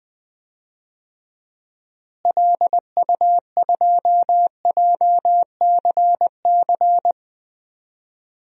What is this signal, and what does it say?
Morse code, heard as a keyed tone, about 20 wpm, 700 Hz. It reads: LU2JCC
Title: DXCC-100 Call Signs - 6 Characters - Repeated 2x - 20wpm